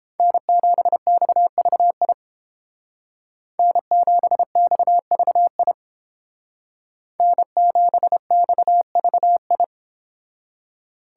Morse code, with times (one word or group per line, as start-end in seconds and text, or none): 0.2-2.1 N7XVI
3.6-5.7 N7XVI
7.2-9.6 N7XVI